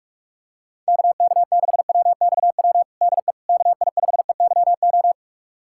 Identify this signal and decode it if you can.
Morse code, heard as a keyed tone, about 46 wpm, 700 Hz. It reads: KK6WXW DE KI5EYO
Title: USA Callsign DE Callsign - 6 Characters - 45wpm